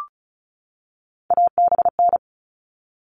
Morse code, heard as a keyed tone, about 34 wpm, 700 Hz. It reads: ABN
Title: All Cut Numbers - 3-Digit 35wpm